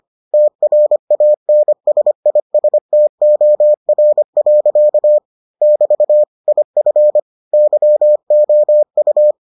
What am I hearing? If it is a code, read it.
Morse code, heard as a keyed tone, about 25 wpm, 600 Hz. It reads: TRANSISTOR. <BT> IF YOU